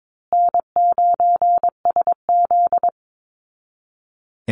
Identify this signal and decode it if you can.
Morse code, heard as a keyed tone, about 22 wpm, 700 Hz. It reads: N9SZ
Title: Easy Call Signs Repeated 2x - 22wpm